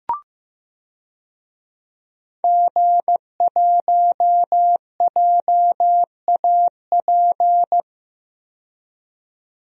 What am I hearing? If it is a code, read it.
Morse code, heard as a keyed tone, about 15 wpm, 700 Hz. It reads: G1JAP
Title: DXCC-100 Call Signs - 5 Characters - 15wpm